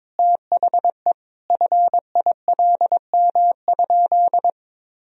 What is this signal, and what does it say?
Morse code, heard as a keyed tone, about 21 wpm, 700 Hz. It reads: THE FILM?
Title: Sentences from Top 1000 Words - Review Part 3 - 22wpm